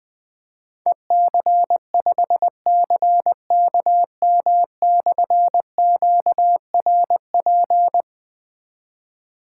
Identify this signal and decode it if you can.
Morse code, heard as a keyed tone, about 20 wpm, 700 Hz. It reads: EC5CKM/QRP
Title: Difficult Call Signs 20wpm